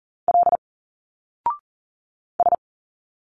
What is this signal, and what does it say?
Morse code, heard as a keyed tone, about 39 wpm, 700 Hz. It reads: L S
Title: Single Letters 40wpm